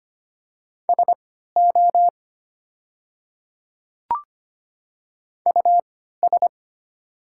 Morse code, heard as a keyed tone, about 25 wpm, spaced as for 14 wpm, 700 Hz. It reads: SO US
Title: 3x Character Spacing - Top 500 Words - 25wpm